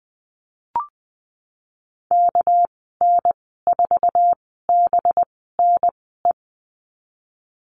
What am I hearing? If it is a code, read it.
Morse code, heard as a keyed tone, about 20 wpm, spaced as for 15 wpm, 700 Hz. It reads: KN4BNE
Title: 238 Lesson 38g - Letter q - Call Signs 20wpm